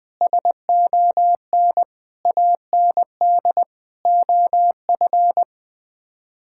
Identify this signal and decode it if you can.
Morse code, heard as a keyed tone, about 19 wpm, 700 Hz. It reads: SON AND OF
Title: Sets of 9 Words - 20wpm